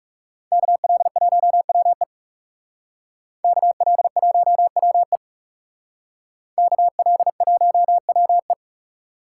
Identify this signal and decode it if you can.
Morse code, heard as a keyed tone, about 38 wpm, 700 Hz. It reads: KL1WE KL1WE KL1WE